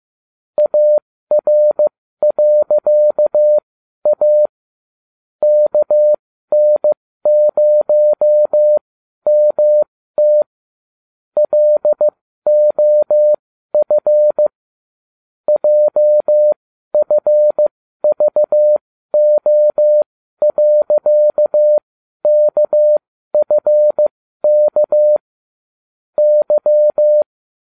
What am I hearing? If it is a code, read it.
Morse code, heard as a keyed tone, about 15 wpm, spaced as for 12 wpm, 600 Hz. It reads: AR.A KN0MT LOF JFVO.KFK Y